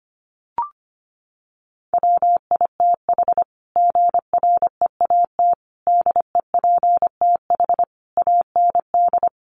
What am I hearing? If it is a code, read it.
Morse code, heard as a keyed tone, about 25 wpm, 700 Hz. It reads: WITH GREAT DEPTH AND
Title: Sentences from Top 1200 Words 25wpm